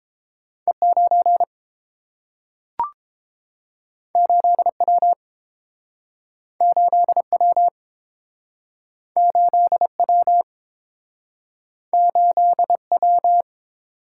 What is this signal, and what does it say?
Morse code, heard as a keyed tone, about 30 wpm, 700 Hz. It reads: E9 8W 8W 8W 8W